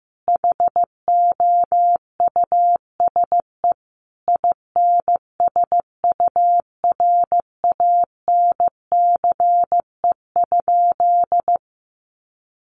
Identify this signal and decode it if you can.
Morse code, heard as a keyed tone, about 15 wpm, 700 Hz. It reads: HOUSE INSURANCE?
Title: Sentences from Top 1100 Words 15wpm